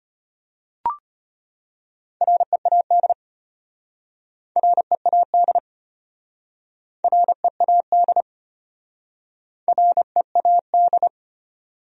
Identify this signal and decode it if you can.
Morse code, heard as a keyed tone, about 33 wpm, 700 Hz. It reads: READ READ READ READ